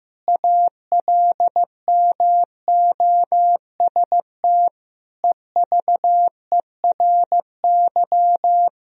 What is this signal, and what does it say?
Morse code, heard as a keyed tone, about 15 wpm, 700 Hz. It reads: ALMOST EVERY
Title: Sentences from Top 1000 Words - Review Part 4 - 15wpm